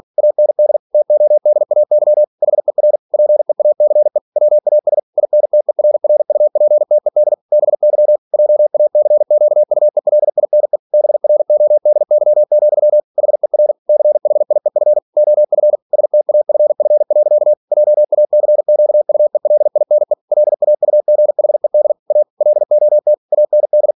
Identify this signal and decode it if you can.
Morse code, heard as a keyed tone, about 46 wpm, 600 Hz. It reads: AND TODAY HER PEACE WAS INTERRUPTED BY JACQUELINE BRODY, HER CHIEF OF STAFF. JACQUELINE LAUGHED A LOT AND
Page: Never 47wpm